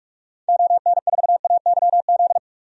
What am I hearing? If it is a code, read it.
Morse code, heard as a keyed tone, about 46 wpm, 700 Hz. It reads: ON4AYZ
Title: DXCC-100 Call Signs - 6 Characters - 45wpm